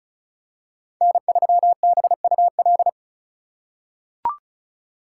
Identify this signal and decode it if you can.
Morse code, heard as a keyed tone, about 36 wpm, 700 Hz. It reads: N3BUL
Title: USA Call Signs - 5 Characters - 4x Character Spacing - 35wpm